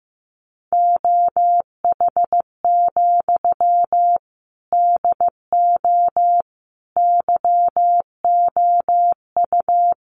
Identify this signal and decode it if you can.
Morse code, heard as a keyed tone, about 15 wpm, 700 Hz. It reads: OH, DO YOU
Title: Sentences from Top 500 Words 15wpm